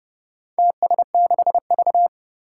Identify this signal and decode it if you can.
Morse code, heard as a keyed tone, about 30 wpm, 700 Hz. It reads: TS6V
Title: Four Challenging Letter-Number Combinations 30wpm